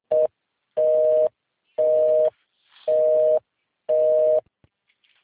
Busy tone (North America), 480 + 620 Hz.
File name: out- -2012- - - .4307